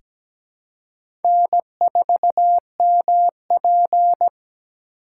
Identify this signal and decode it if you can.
Morse code, heard as a keyed tone, about 17 wpm, 700 Hz. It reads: N4MP
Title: USA Call Signs - 4x Character Spacing - 17wpm